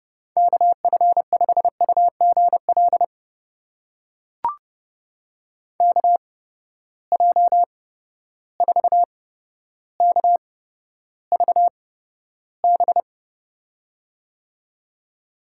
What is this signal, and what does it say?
Morse code, heard as a keyed tone, about 30 wpm, spaced as for 13 wpm, 700 Hz. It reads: KF5UGL KJ4KVB
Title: USA Call Signs - 6 Characters - 8x Character Spacing - 30wpm